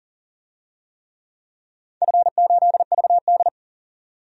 Morse code, heard as a keyed tone, about 40 wpm, 700 Hz. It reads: F8VD
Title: DXCC-100 Call Signs - 4 Characters - 40wpm